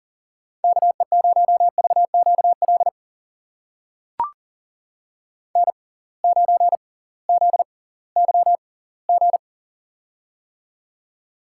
Morse code, heard as a keyed tone, about 40 wpm, spaced as for 22 wpm, 700 Hz. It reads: KE0VQL N9ZYG
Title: USA Call Signs - 6x Character Spacing - 40wpm